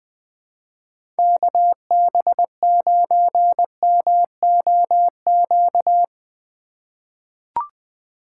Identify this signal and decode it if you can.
Morse code, heard as a keyed tone, about 20 wpm, 700 Hz. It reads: KB9MOQ
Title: USA Call Signs - 3x Character Spacing - 20wpm